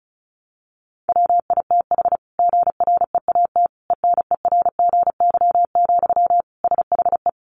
Morse code, heard as a keyed tone, about 34 wpm, 700 Hz. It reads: WITH GREAT ENERGY, SHE
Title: Sentences from Top 900 Words 35wpm